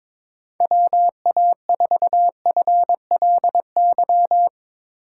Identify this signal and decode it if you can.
Morse code, heard as a keyed tone, about 22 wpm, 700 Hz. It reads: WA4FLY
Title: USA Call Signs - 6 Characters - 4x Character Spacing - 22wpm